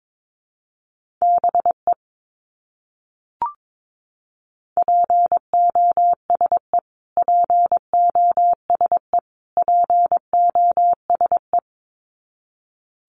Morse code, heard as a keyed tone, about 22 wpm, 700 Hz. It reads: BE POSE POSE POSE
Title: Top 500 Words Repeated 22wpm